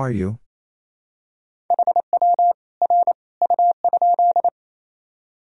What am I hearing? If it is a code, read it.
Morse code, heard as a keyed tone, about 28 wpm, 700 Hz. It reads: HW R U?
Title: Sentences from Top 100 Words and Abbreviations 28wpm